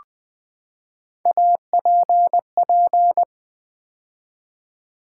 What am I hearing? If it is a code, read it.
Morse code, heard as a keyed tone, about 20 wpm, 700 Hz. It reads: APP
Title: Build A Word - 3 Letters At A Time - 9-Letter Words - 20wpm